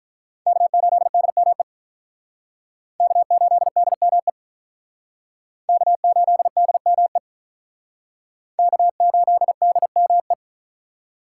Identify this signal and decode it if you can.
Morse code, heard as a keyed tone, about 44 wpm, 700 Hz. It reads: K8DME K8DME K8DME K8DME